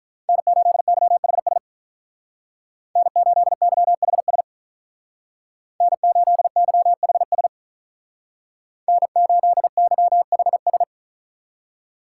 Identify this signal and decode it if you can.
Morse code, heard as a keyed tone, about 43 wpm, 700 Hz. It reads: N8YHS N8YHS N8YHS N8YHS
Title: Speed Racing - USA 1x3 Call Signs 35wpm